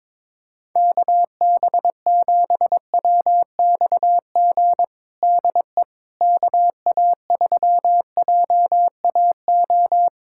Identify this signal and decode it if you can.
Morse code, heard as a keyed tone, about 22 wpm, 700 Hz. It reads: KB7WXG DE KA3JAO
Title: USA Callsign DE Callsign 22wpm